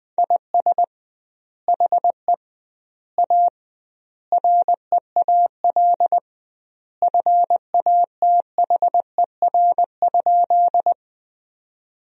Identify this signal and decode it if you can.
Morse code, heard as a keyed tone, about 21 wpm, 700 Hz. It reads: IS HE A REAL FATHER?
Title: 2x Word-Spacing - Sentences from Top 300 Words - 20wpm